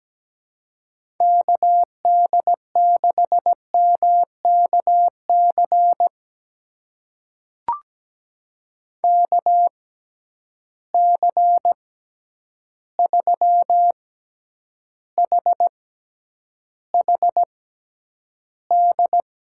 Morse code, heard as a keyed tone, about 17 wpm, spaced as for 9 wpm, 700 Hz. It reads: KD6MKC KC3HHD
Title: USA Call Signs - 6 Characters - 6x Character Spacing - 17wpm